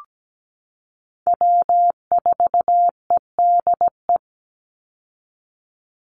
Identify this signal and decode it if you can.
Morse code, heard as a keyed tone, about 17 wpm, 700 Hz. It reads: W4EDE
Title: Call Signs Repeated 2x - 17wpm